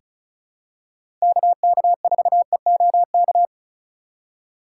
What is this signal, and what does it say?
Morse code, heard as a keyed tone, about 36 wpm, 700 Hz. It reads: KK4EOK
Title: USA Call Signs - 6 Characters - 2x Character Spacing - 35wpm